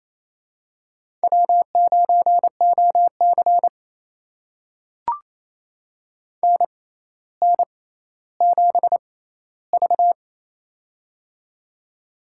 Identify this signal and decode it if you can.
Morse code, heard as a keyed tone, about 28 wpm, spaced as for 14 wpm, 700 Hz. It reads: W9OC NN7V
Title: USA Call Signs - 6x Character Spacing - 28wpm